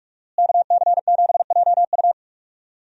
Morse code, heard as a keyed tone, about 41 wpm, 700 Hz. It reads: KC7JU
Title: USA Call Signs - 5 Characters - Repeated 2x - 45wpm